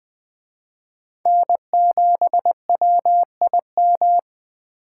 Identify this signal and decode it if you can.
Morse code, heard as a keyed tone, about 21 wpm, 700 Hz. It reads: N7WIM
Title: Speed Racing - USA 1x3 Call Signs 17wpm